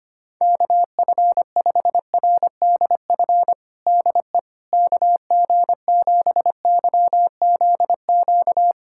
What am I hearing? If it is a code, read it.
Morse code, heard as a keyed tone, about 25 wpm, 700 Hz. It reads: KF5RDF DE KG7YZQ
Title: USA Callsign DE Callsign - 6 Characters - 25wpm